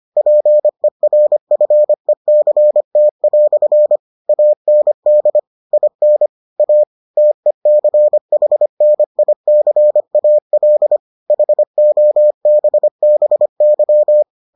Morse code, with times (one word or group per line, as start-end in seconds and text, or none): 0.2-4.0 PERFECT"
4.3-5.4 AND
5.7-6.3 IN
6.6-6.8 A
7.2-11.0 TECHNICAL
11.3-14.2 HOBBY